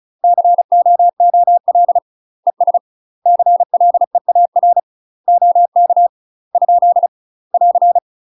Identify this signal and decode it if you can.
Morse code, heard as a keyed tone, about 35 wpm, 700 Hz. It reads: COOL ES CLEAR OK ? <AR>
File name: UK MASTER QSOs TRAINING / 35WPM / UK MASTER QSOs 0012